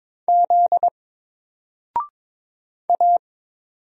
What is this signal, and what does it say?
Morse code, heard as a keyed tone, about 22 wpm, 700 Hz. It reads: Z A
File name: Single-Letter-Number-22wpm